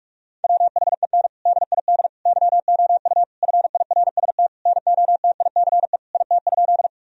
Morse code, heard as a keyed tone, about 46 wpm, 700 Hz. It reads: WHEN DID YOU FIRST NOTICE IT?
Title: Sentences from Top 1000 Words - Review Part 2 - 45wpm